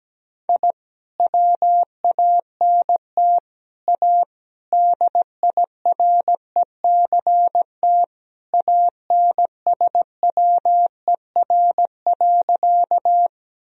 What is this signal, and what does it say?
Morse code, heard as a keyed tone, about 17 wpm, 700 Hz. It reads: I WANT A DIRECT ANSWER.
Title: Sentences from Top 1000 Words - Review Part 2 - 17wpm